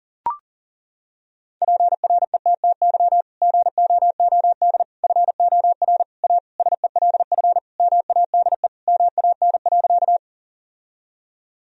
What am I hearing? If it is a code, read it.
Morse code, heard as a keyed tone, about 40 wpm, 700 Hz. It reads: PRETTY GOOD FOR A SELF MADE MAN.
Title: Sentences from Top 1000 Words - Review Part 1 - 40wpm